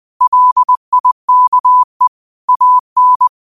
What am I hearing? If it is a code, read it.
Morse code, heard as a keyed tone, about 20 wpm, 1000 Hz. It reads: LIKE AN
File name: qotdfiles / 20wpm